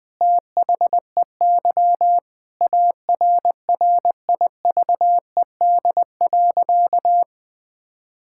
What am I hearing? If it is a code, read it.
Morse code, heard as a keyed tone, about 21 wpm, 700 Hz. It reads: THEY ARRIVED.
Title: Sentences with 7-Word Phrases - 20wpm